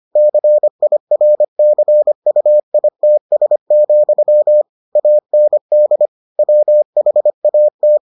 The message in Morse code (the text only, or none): CIRCUITS, AND WHAT